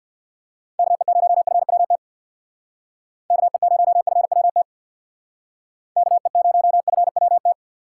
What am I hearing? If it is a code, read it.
Morse code, heard as a keyed tone, about 60 wpm, 700 Hz. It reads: KE0FWT KE0FWT KE0FWT
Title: Speed Racing - USA 2x3 Call Signs 50wpm